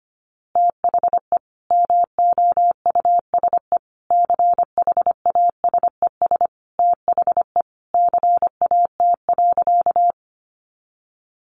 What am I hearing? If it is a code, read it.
Morse code, heard as a keyed tone, about 25 wpm, 700 Hz. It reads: THE MOUSE CHASES THE CAT.
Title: Easy Phrases 25wpm